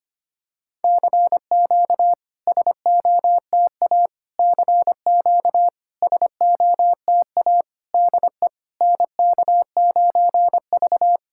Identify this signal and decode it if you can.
Morse code, heard as a keyed tone, about 25 wpm, 700 Hz. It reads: CQ SOTA CQ SOTA DE NK9V
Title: SOTA QSO Practice 25wpm